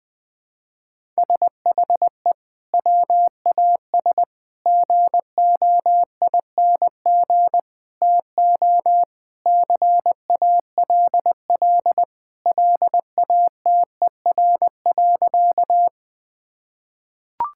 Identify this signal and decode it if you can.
Morse code, heard as a keyed tone, about 21 wpm, 700 Hz. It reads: SHE WAS GOING TO CALL LATER.
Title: Sentences with 4-Word Phrases - 20wpm